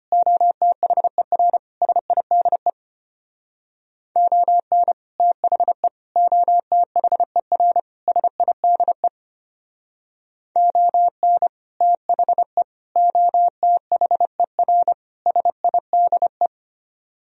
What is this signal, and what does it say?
Morse code, heard as a keyed tone, about 31 wpm, 700 Hz. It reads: OTHER SIDE ON THE OTHER SIDE ON THE OTHER SIDE
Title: Speed-Racing - Sets of 4 Words 25wpm